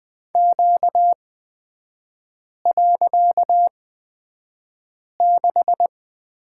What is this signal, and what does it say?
Morse code, heard as a keyed tone, about 20 wpm, spaced as for 5 wpm, 700 Hz. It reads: Q.6
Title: 255 Lesson 41e - Period - Three Characters 20wpm